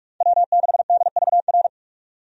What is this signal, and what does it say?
Morse code, heard as a keyed tone, about 46 wpm, 700 Hz. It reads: W6DVR
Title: USA Call Signs - 5 Characters - 45wpm